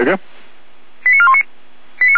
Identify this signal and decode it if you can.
Five-tone selective call (ZVEI), heard as quick runs of five tones, tones of about 70 ms, with speech. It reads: 89419
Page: ZVEI